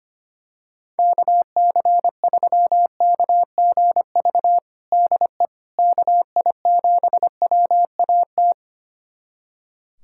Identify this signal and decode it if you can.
Morse code, heard as a keyed tone, about 25 wpm, 700 Hz. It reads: KC3KGV DE KI7WAT
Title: USA Callsign DE Callsign 25wpm